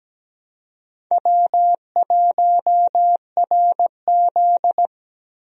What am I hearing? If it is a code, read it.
Morse code, heard as a keyed tone, about 17 wpm, 700 Hz. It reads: W1RZ
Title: USA Call Signs - 4 Characters - Repeated 2x - 17wpm